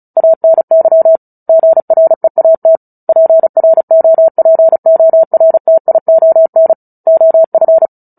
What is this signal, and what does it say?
Morse code, heard as a keyed tone, about 34 wpm, 650 Hz. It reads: ANY GREAT PROPORTION OF
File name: Book Two / 09-Chapter 9 - 35 WPM